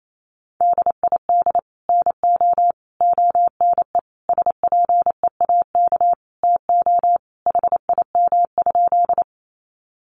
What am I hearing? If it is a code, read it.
Morse code, heard as a keyed tone, about 28 wpm, 700 Hz. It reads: DID NO ONE SPEAK TO HIM?
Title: Sentences from Top 1100 Words 28wpm